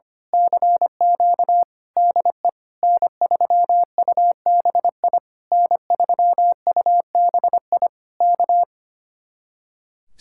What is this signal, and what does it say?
Morse code, heard as a keyed tone, about 24 wpm, 700 Hz. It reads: CQ DE N3UBI N3UBI K
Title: Speed-Racing Common QSO Phrases - 25wpm